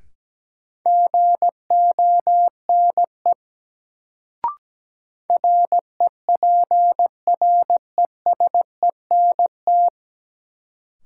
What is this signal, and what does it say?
Morse code, heard as a keyed tone, about 17 wpm, 700 Hz. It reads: GONE REPRESENT
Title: Sentences from Top 1200 Words - New Word Drill Alt - 17wpm